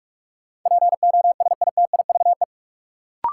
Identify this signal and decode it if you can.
Morse code, heard as a keyed tone, about 45 wpm, 700 Hz. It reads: POSITIVE
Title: Sentences from Top 1400 Words - New Word Drill Alt - 45wpm